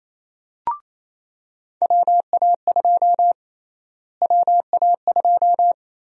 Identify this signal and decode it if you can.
Morse code, heard as a keyed tone, about 28 wpm, 700 Hz. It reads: WA2 WA2
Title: USA Call Sign Prefixes - Repeated 2x - 28wpm